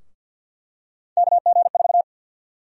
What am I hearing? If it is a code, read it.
Morse code, heard as a keyed tone, about 52 wpm, 700 Hz. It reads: KG4
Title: USA Call Sign Prefixes - 50wpm